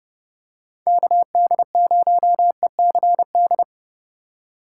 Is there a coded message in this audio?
Yes: Morse code from 0.9 to 3.6 s, heard as a keyed tone, about 29 wpm, 700 Hz. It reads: KD0ECD